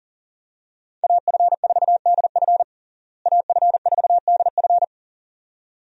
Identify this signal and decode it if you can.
Morse code, heard as a keyed tone, about 40 wpm, 700 Hz. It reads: AF4DF AF4DF
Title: USA Call Signs - 5 Characters - Repeated 2x - 40wpm